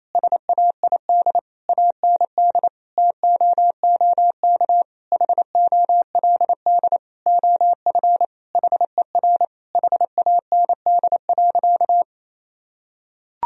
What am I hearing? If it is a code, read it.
Morse code, heard as a keyed tone, about 28 wpm, 700 Hz. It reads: SAID AND TOOK HOLD OF HER HAND.
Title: Sentences from Top 1000 Words - Review Part 2 - 28wpm